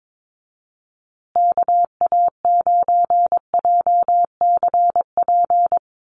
Morse code, heard as a keyed tone, about 22 wpm, 700 Hz. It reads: KA9JCP
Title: USA Call Signs Repeated 2x 22wpm